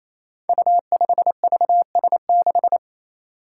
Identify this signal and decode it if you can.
Morse code, heard as a keyed tone, about 29 wpm, 700 Hz. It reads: U5VS6
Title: Five Challenging Letter-Number Combinations 28wpm